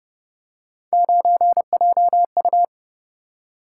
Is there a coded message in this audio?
Yes: Morse code from 0.9 to 2.6 s, heard as a keyed tone, about 29 wpm, 700 Hz. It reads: 9JU